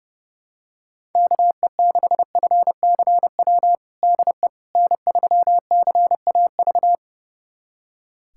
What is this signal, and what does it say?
Morse code, heard as a keyed tone, about 30 wpm, 700 Hz. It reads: KE6FCW DE N3CAV
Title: USA Callsign DE Callsign 30wpm